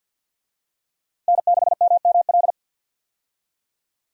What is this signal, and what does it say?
Morse code, heard as a keyed tone, about 50 wpm, 700 Hz. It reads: N6MML
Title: USA Call Signs - 5 Characters - 50wpm